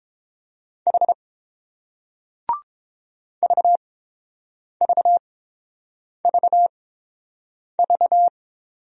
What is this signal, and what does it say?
Morse code, heard as a keyed tone, about 29 wpm, spaced as for 7 wpm, 690 Hz. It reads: H VVVV